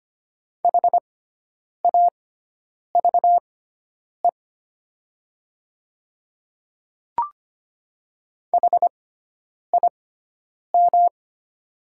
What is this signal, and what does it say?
Morse code, heard as a keyed tone, about 25 wpm, spaced as for 9 wpm, 700 Hz. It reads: HAVE HIM
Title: 6x Character Spacing - Top 100 Words - 25wpm